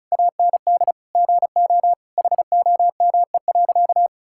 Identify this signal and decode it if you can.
Morse code, heard as a keyed tone, about 34 wpm, 700 Hz. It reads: AND GO HOME.
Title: Sentences from Top 1000 Words - Review Part 6 - 35wpm